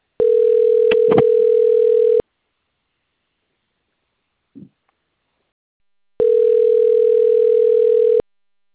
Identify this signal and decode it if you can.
Ringback tone (North America), 440 + 480 Hz.